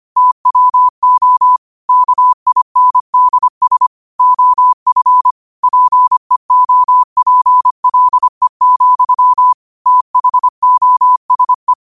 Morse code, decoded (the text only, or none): TWO KINDS OF PEOPLE, THOSE